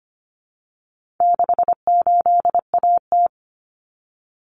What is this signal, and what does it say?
Morse code, heard as a keyed tone, about 25 wpm, 700 Hz. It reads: 68AT
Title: Common Cut Numbers - 4-Digit 25wpm